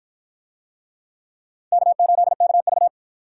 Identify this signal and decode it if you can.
Morse code, heard as a keyed tone, about 53 wpm, 700 Hz. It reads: K8KV